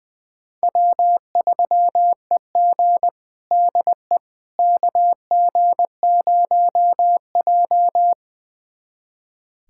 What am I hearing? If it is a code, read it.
Morse code, heard as a keyed tone, about 20 wpm, 700 Hz. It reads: W3EG DE KG0J
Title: USA Callsign DE Callsign - 4 Characters - 20wpm